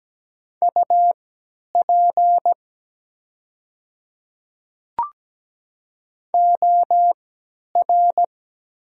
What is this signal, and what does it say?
Morse code, heard as a keyed tone, about 17 wpm, spaced as for 10 wpm, 700 Hz. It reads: UP OR
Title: 3x Character Spacing - Top 200 Words - 17wpm